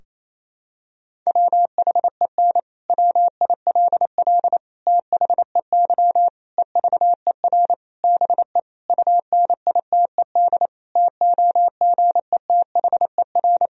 Morse code, heard as a keyed tone, about 28 wpm, 700 Hz. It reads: WHEN WILL THEY EVER BE UNITED TOGETHER
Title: Sentences from Top 1000 Words - Review Part 4 - 28wpm